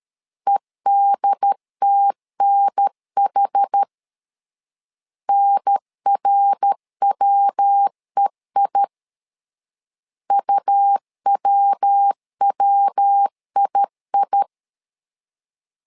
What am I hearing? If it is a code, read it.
Morse code, heard as a keyed tone, about 13 wpm, 800 Hz. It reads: EDTNH NRWEI UWWII
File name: LEZIONI / 4 Q 12wpm 1833